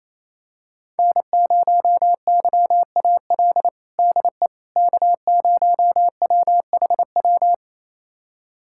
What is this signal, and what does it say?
Morse code, heard as a keyed tone, about 28 wpm, 700 Hz. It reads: N0YAL DE K0WHW
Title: USA Callsign DE Callsign - 5 Characters - 28wpm